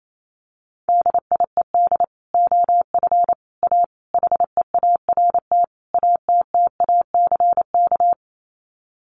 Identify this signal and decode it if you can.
Morse code, heard as a keyed tone, about 28 wpm, 700 Hz. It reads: DIED OF A HEART ATTACK
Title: Sets of 5 Words Encore2 28wpm